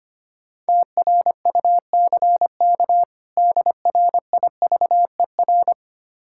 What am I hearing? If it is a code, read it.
Morse code, heard as a keyed tone, about 25 wpm, 700 Hz. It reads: TRUCK DRIVER
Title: K1USN SST QSO Practice 25wpm